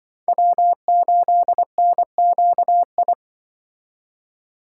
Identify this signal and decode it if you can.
Morse code, heard as a keyed tone, about 24 wpm, 700 Hz. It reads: W8NQI
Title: Speed Racing - USA 1x3 Call Signs 20wpm